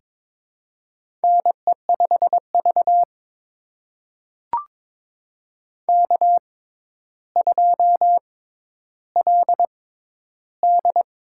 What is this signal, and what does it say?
Morse code, heard as a keyed tone, about 22 wpm, spaced as for 11 wpm, 700 Hz. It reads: NE5V K2LD
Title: USA Call Signs - 4 Characters - 6x Character Spacing - 22wpm